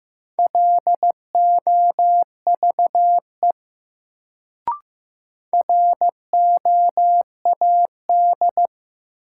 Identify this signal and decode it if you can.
Morse code, heard as a keyed tone, about 15 wpm, 700 Hz. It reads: LOVE ROAD
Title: Sentences from Top 400 Words - New Word Drill - 15wpm